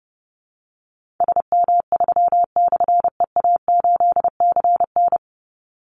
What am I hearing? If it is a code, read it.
Morse code, heard as a keyed tone, about 30 wpm, 700 Hz. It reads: SM3/EA8CN
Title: Call Signs - 30wpm